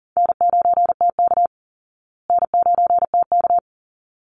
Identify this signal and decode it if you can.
Morse code, heard as a keyed tone, about 40 wpm, 700 Hz. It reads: N9TK N9TK
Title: Easy Call Signs Repeated 2x - 40wpm